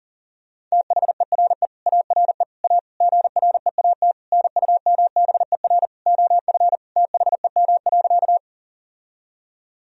Morse code, heard as a keyed tone, about 37 wpm, 700 Hz. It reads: THERE ARE A GREAT NUMBER OF THEM.
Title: Sentences from Top 1000 Words - Review Part 1 - 40wpm